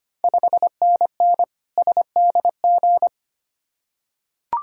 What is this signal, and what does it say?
Morse code, heard as a keyed tone, about 25 wpm, 690 Hz. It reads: 5NN SDG
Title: Ontario QSO Party Exchange 25wpm